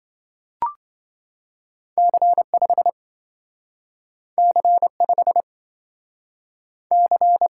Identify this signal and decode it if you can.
Morse code, heard as a keyed tone, about 27 wpm, 700 Hz. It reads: C5 C5 C